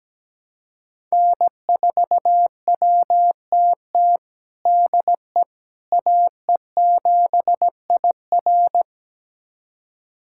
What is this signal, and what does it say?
Morse code, heard as a keyed tone, about 17 wpm, 700 Hz. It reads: N4WTT DE AE7IR
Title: USA Callsign DE Callsign - 5 Characters - 17wpm